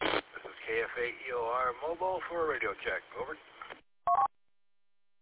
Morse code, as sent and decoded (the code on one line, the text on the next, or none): none